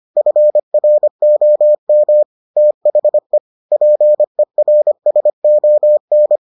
Morse code, heard as a keyed tone, about 25 wpm, 600 Hz. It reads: FROM THE PERSON